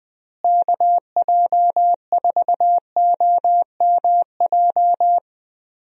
Morse code, heard as a keyed tone, about 21 wpm, 700 Hz. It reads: KJ4OMJ
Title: USA Call Signs - 6 Characters - Repeated 2x - 20wpm